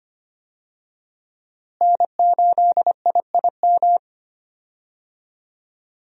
Morse code, heard as a keyed tone, about 25 wpm, 700 Hz. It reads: N8IIM
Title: USA Call Signs - 5 Characters - Repeated 2x - 25wpm